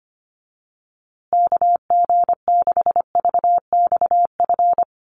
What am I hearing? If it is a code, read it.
Morse code, heard as a keyed tone, about 25 wpm, 700 Hz. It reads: KG6VXF
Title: USA Call Signs Repeated 2x 25wpm